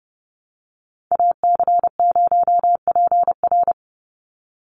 Morse code, heard as a keyed tone, about 30 wpm, 700 Hz. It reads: AC0PR
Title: NA SOTA Activator Callsigns 30wpm